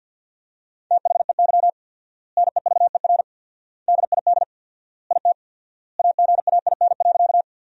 Morse code, heard as a keyed tone, about 51 wpm, 700 Hz. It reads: THEY NEVER DID IT AGAIN.
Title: 4x Word-Spacing - Sentences from Top 300 Words - 50wpm